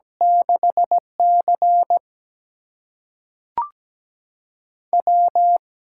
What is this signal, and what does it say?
Morse code, heard as a keyed tone, about 17 wpm, spaced as for 5 wpm, 700 Hz. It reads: 6CW
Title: USA Call Signs - 8x Character Spacing - 17wpm